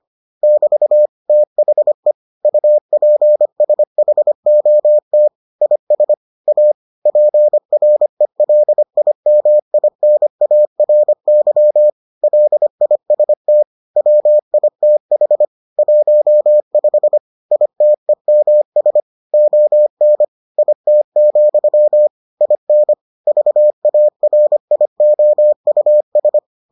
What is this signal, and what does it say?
Morse code, heard as a keyed tone, about 25 wpm, 600 Hz. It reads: <BT> THE UPSHOT IS A PRELIMINARY LIST WITH 15 ITEMS ON IT, IN VARIOUS